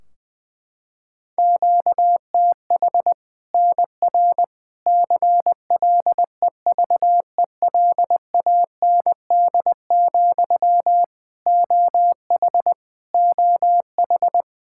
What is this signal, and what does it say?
Morse code, heard as a keyed tone, about 20 wpm, 700 Hz. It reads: QTH NR CLEVELAND, OH OH
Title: K1USN SST QSO Practice 20wpm